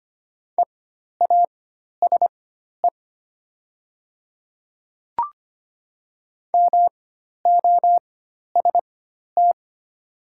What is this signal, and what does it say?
Morse code, heard as a keyed tone, about 25 wpm, spaced as for 12 wpm, 700 Hz. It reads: EASE MOST
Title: 4x Character Spacing - Top 300 Words - 25wpm